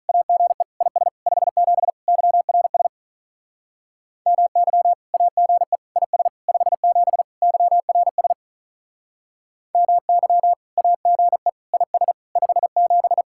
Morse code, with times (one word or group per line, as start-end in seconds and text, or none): 0.1-0.6 AGE
0.8-1.1 IS
1.3-1.9 57
2.1-2.9 YRS
4.3-4.9 MY
5.1-5.8 AGE
6.0-6.3 IS
6.5-7.2 57
7.4-8.3 YRS
9.7-10.5 MY
10.8-11.5 AGE
11.7-12.1 IS
12.4-13.2 57